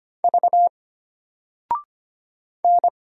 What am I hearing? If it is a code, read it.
Morse code, heard as a keyed tone, about 25 wpm, 700 Hz. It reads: V N
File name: Single-Letter-Number-25wpm